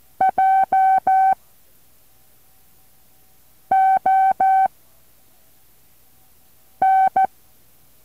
Morse code, heard as a keyed tone, about 14 wpm, 770 Hz. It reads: J O N
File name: intl morse code2